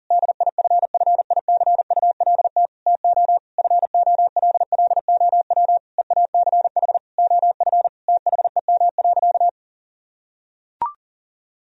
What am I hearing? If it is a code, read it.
Morse code, heard as a keyed tone, about 37 wpm, 700 Hz. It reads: DIFFICULT TO FOLLOW EACH OF THEM.
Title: Sentences from Top 1000 Words - Review Part 1 - 40wpm